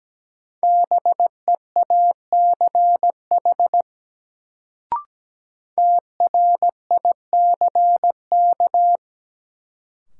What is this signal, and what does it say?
Morse code, heard as a keyed tone, about 17 wpm, 700 Hz. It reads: BEACH TRICK
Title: Sentences from Top 1500 Words - New Word Drill - 17wpm